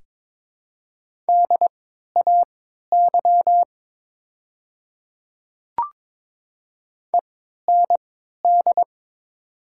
Morse code, heard as a keyed tone, about 22 wpm, spaced as for 13 wpm, 690 Hz. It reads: DAY END
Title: 3x Character Spacing - Top 300 Words - 22wpm